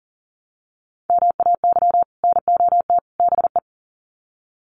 Morse code, heard as a keyed tone, about 39 wpm, 700 Hz. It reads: MAY NOT BE
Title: Sets of 3 Words 40wpm